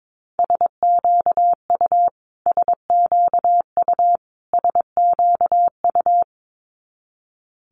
Morse code, heard as a keyed tone, about 22 wpm, 700 Hz. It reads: SQU SQU SQU
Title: Multi-letter Phonemes Repeated 3x 22wpm